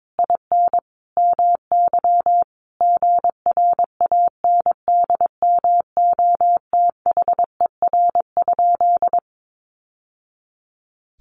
Morse code, heard as a keyed tone, about 22 wpm, 700 Hz. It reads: IN MY GRANDMOTHER?
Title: Sentences from Top 1200 Words 22wpm